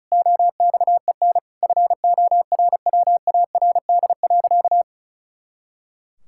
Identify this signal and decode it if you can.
Morse code, heard as a keyed tone, about 36 wpm, 700 Hz. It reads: OXEN FORWARD.
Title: Pangram Challenge v2 - 35wpm